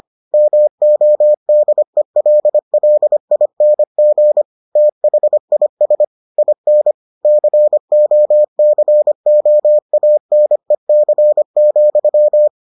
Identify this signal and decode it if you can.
Morse code, heard as a keyed tone, about 25 wpm, 600 Hz. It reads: MODELLING THIS IN COCOANEC,